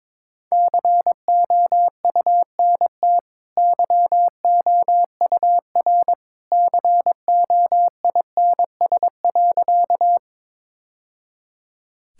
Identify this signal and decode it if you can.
Morse code, heard as a keyed tone, about 22 wpm, 690 Hz. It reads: COUNT YOUR COINS.
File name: practice / 22wpm / Easy Sentences with Hints - 22wpm